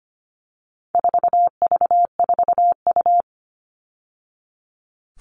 Morse code, heard as a keyed tone, about 25 wpm, 700 Hz. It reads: 4V4U
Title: All Cut Numbers - 4-Digit 25wpm